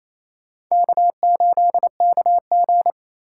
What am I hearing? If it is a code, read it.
Morse code, heard as a keyed tone, about 28 wpm, 700 Hz. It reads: K8KG
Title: USA Call Signs - 4 Characters - 3x Character Spacing - 28wpm